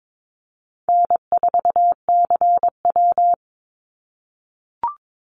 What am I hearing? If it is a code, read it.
Morse code, heard as a keyed tone, about 22 wpm, 700 Hz. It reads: N4CW
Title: Call Signs - 22wpm